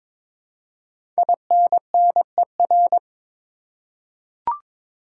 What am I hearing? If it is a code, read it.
Morse code, heard as a keyed tone, about 22 wpm, 700 Hz. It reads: INNER
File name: 5 Letter Words - First 3 Letters Given - 22wpm